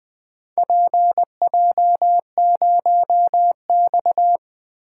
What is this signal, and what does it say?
Morse code, heard as a keyed tone, about 20 wpm, 700 Hz. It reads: PJ0X
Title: Four Letter-Number Combinations - 20wpm